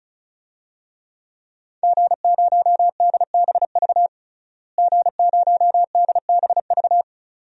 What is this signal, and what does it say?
Morse code, heard as a keyed tone, about 36 wpm, 700 Hz. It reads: G0DBV G0DBV
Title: DXCC-100 Call Signs - 5 Characters - Repeated 2x - 35wpm